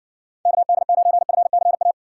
Morse code, heard as a keyed tone, about 56 wpm, 700 Hz. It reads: KD9FCA